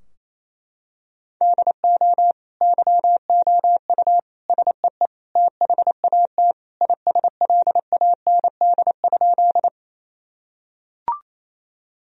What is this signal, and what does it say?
Morse code, heard as a keyed tone, about 28 wpm, 700 Hz. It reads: DO YOU SEE THAT ISLAND?
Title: Sentences from Top 1000 Words - Review Part 3 - 28wpm